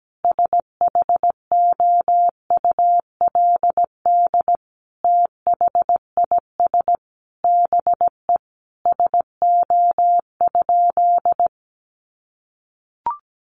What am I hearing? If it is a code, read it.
Morse code, heard as a keyed tone, about 17 wpm, 700 Hz. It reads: SHOULD THIS BE SO?
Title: Sentences from Top 300 Words 17wpm